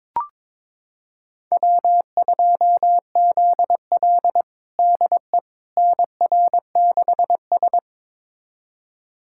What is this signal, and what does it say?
Morse code, heard as a keyed tone, about 22 wpm, 700 Hz. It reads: W2ZL DE NR6S
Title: USA Callsign DE Callsign - 4 Characters - 22wpm